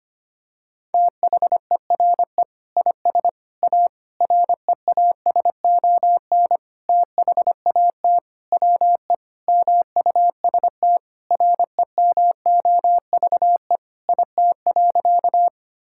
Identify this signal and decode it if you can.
Morse code, heard as a keyed tone, about 25 wpm, 700 Hz. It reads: THERE IS A REASON THAT WE MUST REMOVE IT.
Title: Sentences from Top 1000 Words - Review Part 5 - 25wpm